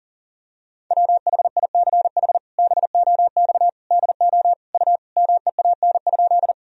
Morse code, heard as a keyed tone, about 37 wpm, 700 Hz. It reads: WHICH BOX DO U MEAN?
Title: Sentences from Top 500 Words and Abbreviations 40wpm